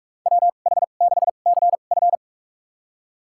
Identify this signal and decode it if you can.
Morse code, heard as a keyed tone, about 46 wpm, spaced as for 33 wpm, 700 Hz. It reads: WH6CF
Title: USA Call Signs - 5 Characters - 2x Character Spacing - 45wpm